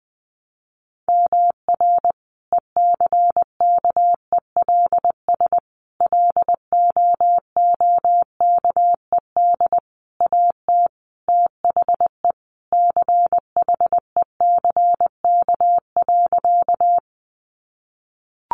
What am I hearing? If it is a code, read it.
Morse code, heard as a keyed tone, about 20 wpm, 700 Hz. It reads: MR ECKELS LOOKED AT THE CHECK.
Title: A Sound Of Thunder 20wpm